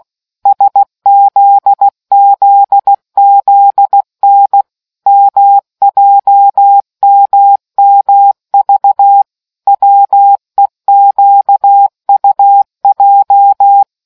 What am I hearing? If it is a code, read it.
Morse code, heard as a keyed tone, about 16 wpm, 800 Hz. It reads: SZZZN MJMMV WEQUJ